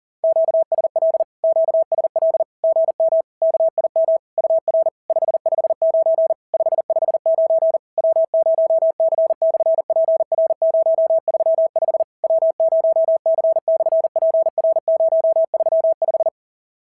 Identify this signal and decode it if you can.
Morse code, heard as a keyed tone, about 37 wpm, 660 Hz. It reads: QSL QSL GM KIM UR 559 559 W0C/PR035 W0C/PR035
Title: SOTA QSO Practice 40wpm